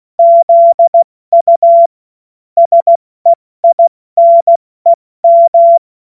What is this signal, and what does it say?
Morse code, heard as a keyed tone, about 16 wpm, 680 Hz. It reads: ZU SEINEM
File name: Morsekurs / Witze / Witz 02. Mai 2023 70 F80